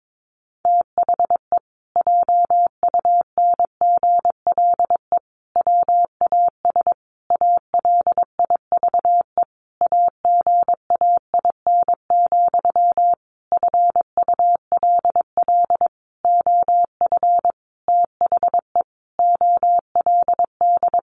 Morse code, heard as a keyed tone, about 22 wpm, 700 Hz. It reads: THE JUNGLE WAS ALIVE AGAIN, FULL OF THE OLD
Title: A Sound Of Thunder 22wpm